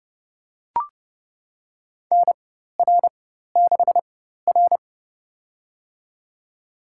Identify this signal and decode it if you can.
Morse code, heard as a keyed tone, about 29 wpm, spaced as for 14 wpm, 700 Hz. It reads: NR6R
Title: USA Call Signs - 4x Character Spacing - 30wpm